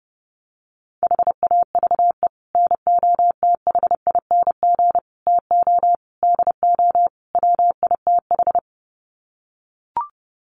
Morse code, heard as a keyed tone, about 30 wpm, 700 Hz. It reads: HAVE NOTHING TO DO WITH
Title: Sets of 5 Words 30wpm